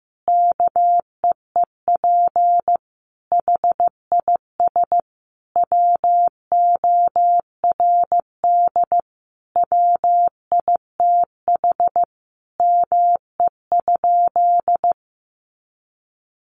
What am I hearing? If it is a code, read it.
Morse code, heard as a keyed tone, about 15 wpm, 700 Hz. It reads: KEEP HIS WORD WITH ME?
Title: Sentences from Top 300 Words 15wpm